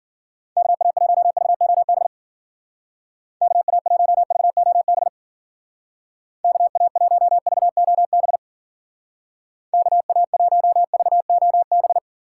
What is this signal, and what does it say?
Morse code, heard as a keyed tone, about 48 wpm, 700 Hz. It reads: KA1VOB KA1VOB KA1VOB KA1VOB
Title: Speed Racing - USA 2x3 Call Signs 40wpm